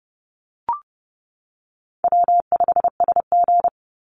Morse code, heard as a keyed tone, about 30 wpm, 700 Hz. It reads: W5SG
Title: Easy Call Signs - 30wpm